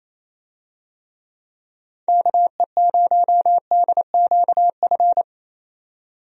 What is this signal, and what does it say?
Morse code, heard as a keyed tone, about 28 wpm, 700 Hz. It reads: KE0DQF
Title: USA Call Signs - 6 Characters - 28wpm